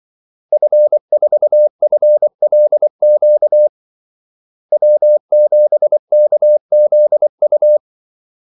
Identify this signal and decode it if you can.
Morse code, heard as a keyed tone, about 24 wpm, 600 Hz. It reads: F4FLQ W7KZU
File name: Callsigns-24wpm